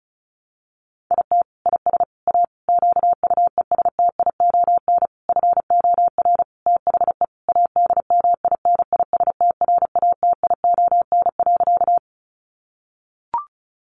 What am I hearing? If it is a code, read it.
Morse code, heard as a keyed tone, about 34 wpm, 700 Hz. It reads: IT IS A QUESTION FOR THE ADMINISTRATION.
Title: Sentences from Top 1300 Words 35wpm